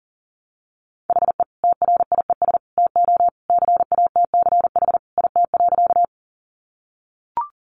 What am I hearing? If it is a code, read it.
Morse code, heard as a keyed tone, about 37 wpm, 700 Hz. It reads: HE TRIES TO CATCH IT.
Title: Additional Easy Phrases 40wpm